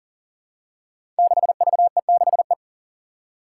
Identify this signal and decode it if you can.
Morse code, heard as a keyed tone, about 40 wpm, 700 Hz. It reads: 6VE6E
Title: Five Challenging Letter-Number Combinations 40wpm